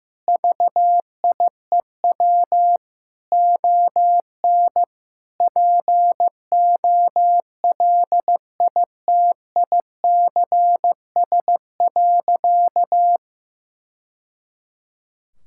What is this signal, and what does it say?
Morse code, heard as a keyed tone, about 15 wpm, 700 Hz. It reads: VIEW ON POLITICS.
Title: Sentences from Top 1000 Words - Review Part 6 - 15wpm